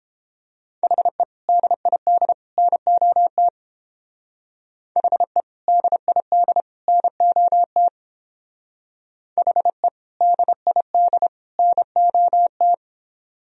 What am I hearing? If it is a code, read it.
Morse code, heard as a keyed tone, about 28 wpm, 700 Hz. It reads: HE DID NOT HE DID NOT HE DID NOT